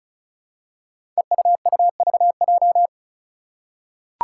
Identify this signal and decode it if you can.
Morse code, heard as a keyed tone, about 34 wpm, 700 Hz. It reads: EUUVJ
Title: Five Challenging Letter-Number Combinations 35wpm